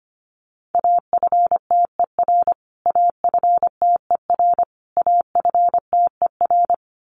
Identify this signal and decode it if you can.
Morse code, heard as a keyed tone, about 25 wpm, 700 Hz. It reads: AFTER AFTER AFTER
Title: Top 500 Words Repeated 25wpm